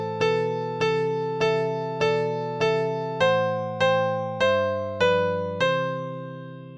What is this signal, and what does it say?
There is no code.